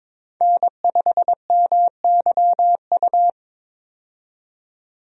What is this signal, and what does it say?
Morse code, heard as a keyed tone, about 22 wpm, 700 Hz. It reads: N5MYU
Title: USA Call Signs - 5 Characters - 22wpm